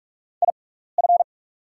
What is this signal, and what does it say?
Morse code, heard as a keyed tone, about 46 wpm, spaced as for 16 wpm, 690 Hz. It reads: IF
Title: 6x Character Spacing - Top 100 Words - 45wpm